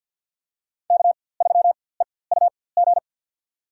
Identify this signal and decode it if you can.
Morse code, heard as a keyed tone, about 47 wpm, spaced as for 23 wpm, 700 Hz. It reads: K3EUG
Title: USA Call Signs - 4x Character Spacing - 50wpm